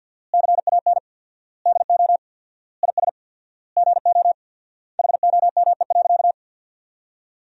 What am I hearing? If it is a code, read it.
Morse code, heard as a keyed tone, about 50 wpm, 700 Hz. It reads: CAN DO IS GO HOME.
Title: 4x Word-Spacing - Sentences from Top 200 Words - 50wpm